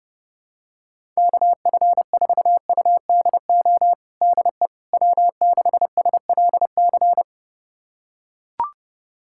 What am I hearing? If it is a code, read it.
Morse code, heard as a keyed tone, about 30 wpm, 700 Hz. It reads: KF4UDO DE W6SLC
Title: USA Callsign DE Callsign 30wpm